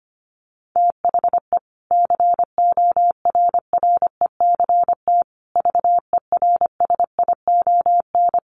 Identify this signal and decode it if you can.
Morse code, heard as a keyed tone, about 25 wpm, 700 Hz. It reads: THE CORRECT VERSION
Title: Sentences from Top 1400 Words 25wpm